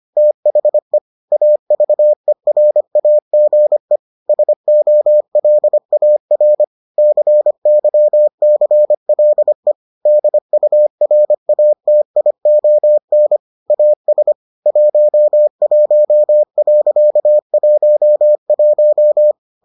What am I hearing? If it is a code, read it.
Morse code, heard as a keyed tone, about 25 wpm, 600 Hz. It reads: THE AVERAGE SOLAR CYCLE DURATION AS 11.11